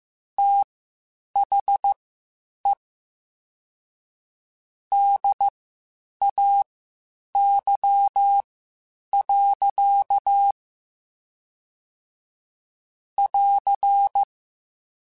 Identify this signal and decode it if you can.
Morse code, heard as a keyed tone, about 15 wpm, spaced as for 8 wpm, 800 Hz. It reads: THE DAY. <AR>
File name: Spaced Characters / Spaced Characters-mp3s / Weather 5 12 sp @13